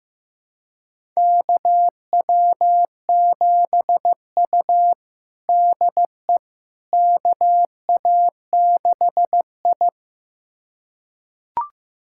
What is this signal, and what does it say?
Morse code, heard as a keyed tone, about 15 wpm, 700 Hz. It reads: KW7U DE KA6I
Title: USA Callsign DE Callsign - 4 Characters - 15wpm